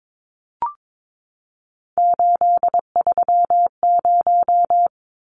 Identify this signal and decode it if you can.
Morse code, heard as a keyed tone, about 22 wpm, 700 Hz. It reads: 830
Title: Numbers - 3-Digit 22wpm